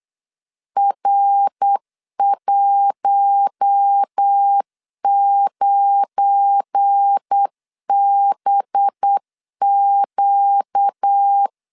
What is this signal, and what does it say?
Morse code, heard as a keyed tone, about 8 wpm, 800 Hz. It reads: R19BQ
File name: LEZIONI / 7 GF9Z / 7 GF9 8wpm 3545